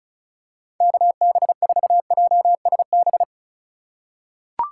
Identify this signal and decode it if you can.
Morse code, heard as a keyed tone, about 34 wpm, 700 Hz. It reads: KB4JSB
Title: USA Call Signs - 6 Characters - 4x Character Spacing - 35wpm